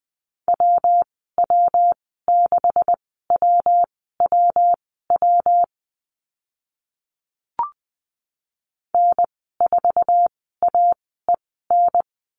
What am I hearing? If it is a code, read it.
Morse code, heard as a keyed tone, about 20 wpm, spaced as for 15 wpm, 700 Hz. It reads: WW6WWW N4AEN
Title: 140 Lesson 23g - Letter f - Call Signs 20wpm